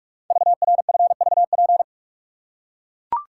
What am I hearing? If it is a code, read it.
Morse code, heard as a keyed tone, about 44 wpm, 700 Hz. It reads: VRFVP
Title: Code Groups - 5 Letters - 45wpm